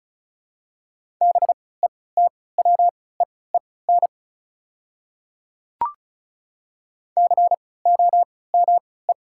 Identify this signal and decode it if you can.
Morse code, heard as a keyed tone, about 34 wpm, spaced as for 20 wpm, 700 Hz. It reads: BETWEEN COME
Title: 3x Character Spacing - Top 500 Words - 35wpm